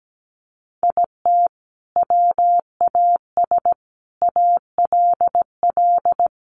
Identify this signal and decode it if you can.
Morse code, heard as a keyed tone, about 17 wpm, 700 Hz. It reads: IT WAS ALL
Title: Sentences from Top 1000 Words 17wpm